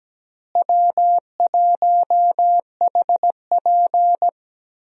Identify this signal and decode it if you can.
Morse code, heard as a keyed tone, about 17 wpm, 700 Hz. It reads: W1HP
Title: USA Call Signs - 4 Characters - Repeated 3x - 17wpm